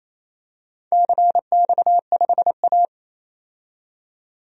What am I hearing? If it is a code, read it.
Morse code, heard as a keyed tone, about 28 wpm, 700 Hz. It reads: CX5A
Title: DXCC-100 Call Signs - 4 Characters - 28wpm